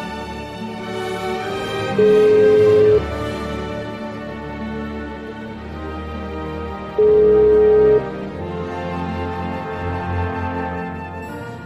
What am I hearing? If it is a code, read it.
Ringback tone (Europe, 425 Hz).